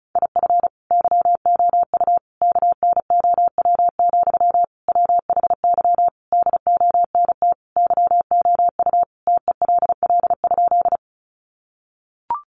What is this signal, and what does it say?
Morse code, heard as a keyed tone, about 34 wpm, 700 Hz. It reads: IF YOU KNOW, WHY DONT YOU TELL?
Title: Sentences from Top 300 Words 35wpm